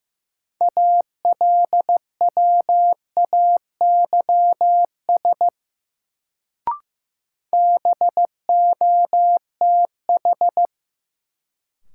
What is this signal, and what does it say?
Morse code, heard as a keyed tone, about 15 wpm, 700 Hz. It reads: ALWAYS BOTH
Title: Sentences from Top 300 Words - New Word Drill - 15wpm